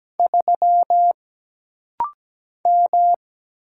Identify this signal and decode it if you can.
Morse code, heard as a keyed tone, about 17 wpm, 700 Hz. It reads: 3 M
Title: Single Letter-Number Alt - Rapid-Fire 17wpm